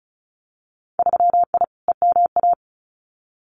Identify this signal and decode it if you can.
Morse code, heard as a keyed tone, about 36 wpm, 700 Hz. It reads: 3I EMA
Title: Winter Field Day Exchanges Encore 35wpm